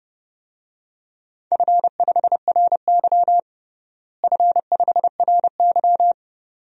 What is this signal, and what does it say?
Morse code, heard as a keyed tone, about 30 wpm, 700 Hz. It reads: F5RY F5RY
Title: DXCC-100 Call Signs - 4 Characters - Repeated 2x - 30wpm